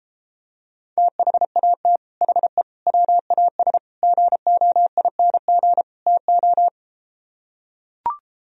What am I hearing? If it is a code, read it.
Morse code, heard as a keyed tone, about 33 wpm, 700 Hz. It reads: THAT HE WAS GOING TO